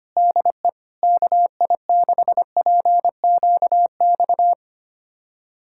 Morse code, heard as a keyed tone, about 25 wpm, 700 Hz. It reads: DE KI6PQX
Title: USA Callsign DE Callsign - 6 Characters - 25wpm